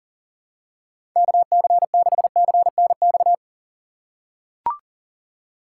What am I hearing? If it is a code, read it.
Morse code, heard as a keyed tone, about 40 wpm, 700 Hz. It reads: KC6CNX
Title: USA Call Signs - 6 Characters - 6x Character Spacing - 40wpm